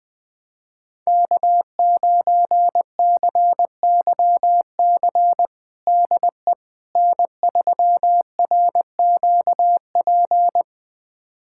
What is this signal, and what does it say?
Morse code, heard as a keyed tone, about 20 wpm, 700 Hz. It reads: K9CYC DE N3RQP
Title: USA Callsign DE Callsign - 5 Characters - 20wpm